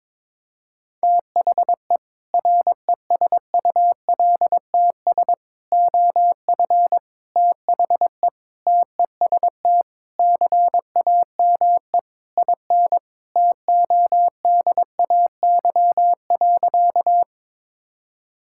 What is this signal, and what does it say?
Morse code, heard as a keyed tone, about 22 wpm, 700 Hz. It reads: THE RESULTS OF THE TEST CAME IN TODAY.
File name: practice / 22wpm / Sentences with 4-Word Phrases - 22wpm